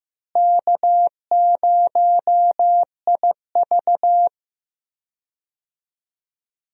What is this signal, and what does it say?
Morse code, heard as a keyed tone, about 15 wpm, 700 Hz. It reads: K0IV
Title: USA Call Signs - 4 Characters - Repeated 2x - 15wpm